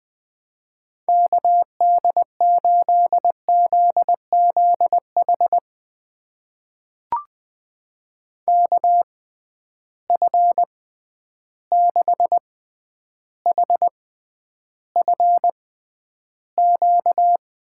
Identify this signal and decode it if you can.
Morse code, heard as a keyed tone, about 21 wpm, spaced as for 10 wpm, 700 Hz. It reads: KD8ZZH KF6HFQ
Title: USA Call Signs - 6 Characters - 6x Character Spacing - 20wpm